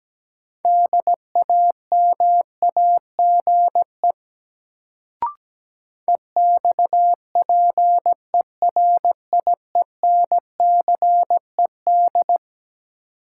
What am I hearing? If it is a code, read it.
Morse code, heard as a keyed tone, about 17 wpm, 700 Hz. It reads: DAMAGE EXPERIENCED
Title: Sentences from Top 1100 Words - New Word Drill - 17wpm